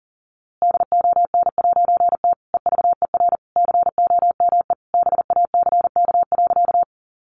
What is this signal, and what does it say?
Morse code, heard as a keyed tone, about 40 wpm, 700 Hz. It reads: DON'T EVER COME BACK.
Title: A Sound Of Thunder 40wpm